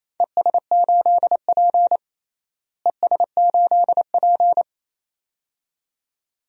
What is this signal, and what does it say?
Morse code, heard as a keyed tone, about 27 wpm, 700 Hz. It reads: ES8P ES8P
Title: DXCC-100 Call Signs - 4 Characters - Repeated 2x - 28wpm